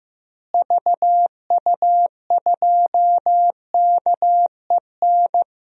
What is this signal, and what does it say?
Morse code, heard as a keyed tone, about 16 wpm, 700 Hz. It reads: VU2KEN
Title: DXCC-100 Call Signs - 6 Characters - Repeated 3x - 15wpm